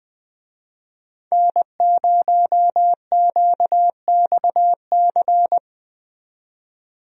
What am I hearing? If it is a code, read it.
Morse code, heard as a keyed tone, about 20 wpm, 700 Hz. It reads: N0QXC
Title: USA Call Signs - 5 Characters - 20wpm